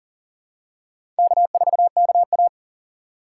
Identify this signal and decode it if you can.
Morse code, heard as a keyed tone, about 39 wpm, 690 Hz. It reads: K4KA
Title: USA Call Signs - 4 Characters - 2x Character Spacing - 40wpm